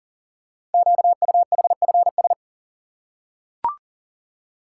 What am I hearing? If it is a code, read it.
Morse code, heard as a keyed tone, about 38 wpm, 700 Hz. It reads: QUHFS
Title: Code Groups - 5 Letters - 40wpm